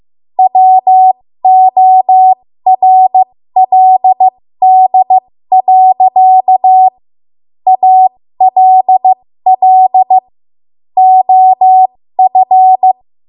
Morse code, heard as a keyed tone, about 15 wpm, spaced as for 13 wpm, 750 Hz. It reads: WORLD. ALL OF
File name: Archive / 13 WPM / 13WPM